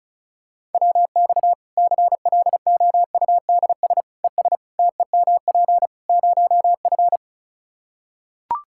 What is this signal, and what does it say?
Morse code, heard as a keyed tone, about 34 wpm, 700 Hz. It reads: WX CLOUDS ES TEMP 0F
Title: Practice QSO - Exchange 2 - Info - 35wpm